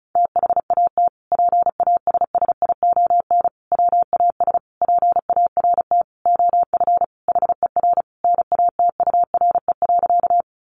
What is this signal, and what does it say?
Morse code, heard as a keyed tone, about 34 wpm, 700 Hz. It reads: THAT PASSION WAS PART OF HER NATURE.
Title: Sentences from Top 1300 Words 35wpm